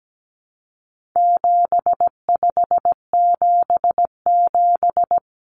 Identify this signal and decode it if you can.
Morse code, heard as a keyed tone, about 17 wpm, 700 Hz. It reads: 7577
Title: Common Cut Numbers - 4-Digit 17wpm